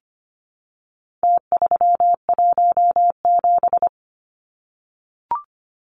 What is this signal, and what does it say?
Morse code, heard as a keyed tone, about 25 wpm, 700 Hz. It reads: T317
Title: Numbers - Contest Style - 4-Digit 25wpm